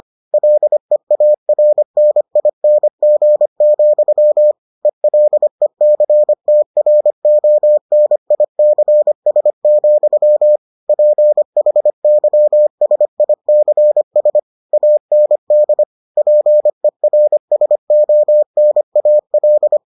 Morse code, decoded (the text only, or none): LEARNING, ELECTRONICS, PHYSICS AND PERSONAL